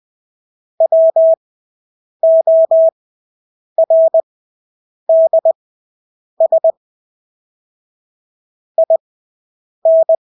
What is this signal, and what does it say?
Morse code, heard as a keyed tone, about 20 wpm, spaced as for 8 wpm, 650 Hz. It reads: WORDS IN
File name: Engelske ord 20/8 wpm